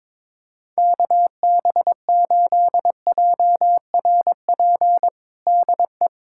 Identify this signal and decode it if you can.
Morse code, heard as a keyed tone, about 22 wpm, 700 Hz. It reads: KB8JRP DE
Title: USA Callsign DE Callsign - 6 Characters - 22wpm